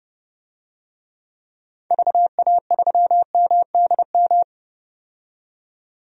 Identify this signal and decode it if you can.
Morse code, heard as a keyed tone, about 30 wpm, 700 Hz. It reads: VA3MDM
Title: DXCC-100 Call Signs - 6 Characters - 30wpm